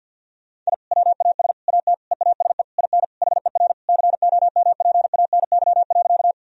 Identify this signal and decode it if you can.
Morse code, heard as a keyed tone, about 46 wpm, 700 Hz. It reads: I WAS AT EASE IN HER COMPANY.
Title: Sentences from Top 1000 Words - Review Part 1 - 50wpm